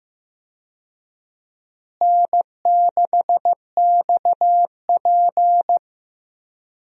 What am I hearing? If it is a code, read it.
Morse code, heard as a keyed tone, about 15 wpm, 700 Hz. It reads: N6XP
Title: USA Call Signs - 4 Characters - Repeated 2x - 15wpm